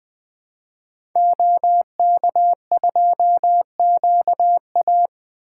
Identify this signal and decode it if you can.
Morse code, heard as a keyed tone, about 21 wpm, 700 Hz. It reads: OK2QA
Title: 2023 SOTA Activator Call Signs - 20wpm